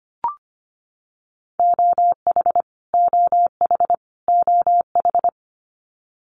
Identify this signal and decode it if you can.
Morse code, heard as a keyed tone, about 25 wpm, 700 Hz. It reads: OH OH OH
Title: Top 500 Words Repeated 25wpm